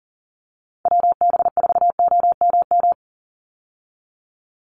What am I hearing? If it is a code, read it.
Morse code, heard as a keyed tone, about 40 wpm, 700 Hz. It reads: WB4OMM
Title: Call Signs Repeated 3x - 40wpm